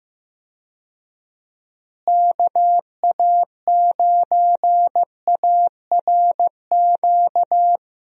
Morse code, heard as a keyed tone, about 15 wpm, 700 Hz. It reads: KA9ARQ
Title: USA Call Signs - 6 Characters - Repeated 2x - 15wpm